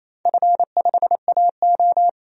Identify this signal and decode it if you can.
Morse code, heard as a keyed tone, about 28 wpm, 690 Hz. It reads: F5AO
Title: DXCC-100 Call Signs - 4 Characters - 28wpm